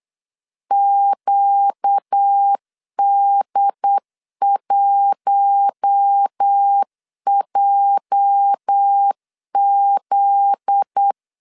Morse code, heard as a keyed tone, about 8 wpm, 800 Hz. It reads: QD1JZ
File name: LEZIONI / 8 YJ45 / 8 YJ45 8wpm 3214